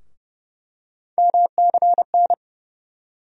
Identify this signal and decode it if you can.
Morse code, heard as a keyed tone, about 30 wpm, 700 Hz. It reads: MCN
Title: USA Call Sign Postfixes - 3 Letters 30wpm